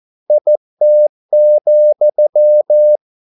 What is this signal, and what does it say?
Morse code, heard as a keyed tone, about 14 wpm, 600 Hz. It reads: IT,